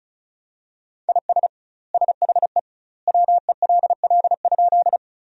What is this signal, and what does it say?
Morse code, heard as a keyed tone, about 36 wpm, 700 Hz. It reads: IS SHE WELL?
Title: 2x Word-Spacing - Sentences from Top 200 Words - 35wpm